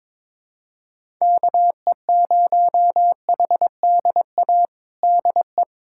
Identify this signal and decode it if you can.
Morse code, heard as a keyed tone, about 22 wpm, 700 Hz. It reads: KE0HDA DE
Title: USA Callsign DE Callsign - 6 Characters - 22wpm